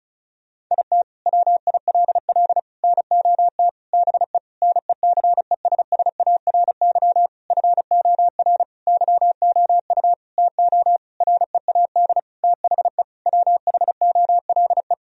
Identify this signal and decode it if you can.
Morse code, heard as a keyed tone, about 34 wpm, 700 Hz. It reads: IT WILL NOT BE NECESSARY FOR YOU TO READ THE WHOLE
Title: Sentences from Top 1000 Words - Review Part 1 - 35wpm